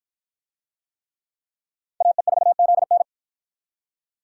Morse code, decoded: AE4ZN

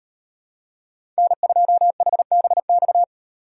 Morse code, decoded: N2HBX

